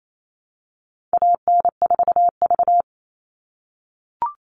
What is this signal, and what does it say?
Morse code, heard as a keyed tone, about 28 wpm, 700 Hz. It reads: AN4V
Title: All Cut Numbers - 4-Digit 28wpm